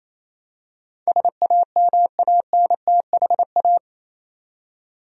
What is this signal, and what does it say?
Morse code, heard as a keyed tone, about 27 wpm, 700 Hz. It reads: SAMANTHA
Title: USA Common Names - 28wpm